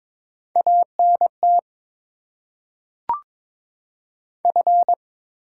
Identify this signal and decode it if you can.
Morse code, heard as a keyed tone, about 22 wpm, 700 Hz. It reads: ANT F
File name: Top 20 QSO Elements - 6x Character Spacing - 22wpm